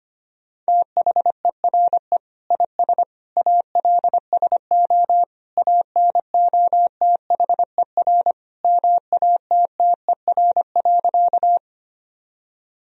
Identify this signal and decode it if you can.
Morse code, heard as a keyed tone, about 25 wpm, 700 Hz. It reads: THERE IS ALSO ANOTHER MATTER.
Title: Sentences from Top 1000 Words - Review Part 1 - 25wpm